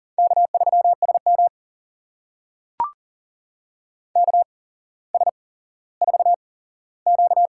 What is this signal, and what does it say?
Morse code, heard as a keyed tone, about 39 wpm, spaced as for 17 wpm, 700 Hz. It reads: K3SM KS4Q
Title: USA Call Signs - 4 Characters - 8x Character Spacing - 40wpm